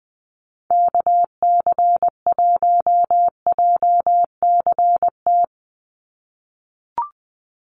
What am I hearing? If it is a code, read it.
Morse code, heard as a keyed tone, about 20 wpm, 700 Hz. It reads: KC1JCT
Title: USA Call Signs 20wpm